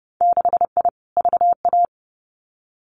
Morse code, heard as a keyed tone, about 30 wpm, 700 Hz. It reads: 6I VA
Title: Winter Field Day Exchanges Encore 30wpm